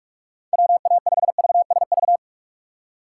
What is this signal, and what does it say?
Morse code, heard as a keyed tone, about 45 wpm, 700 Hz. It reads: WA5VSV